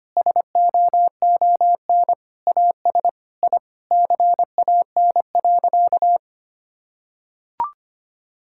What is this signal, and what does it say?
Morse code, heard as a keyed tone, about 25 wpm, 700 Hz. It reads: SOON AS I CAN.
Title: Sentences from Top 1000 Words - Review Part 5 - 25wpm